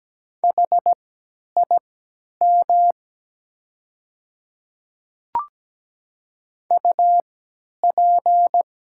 Morse code, heard as a keyed tone, about 17 wpm, spaced as for 10 wpm, 700 Hz. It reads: HIM UP
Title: 3x Character Spacing - Top 200 Words - 17wpm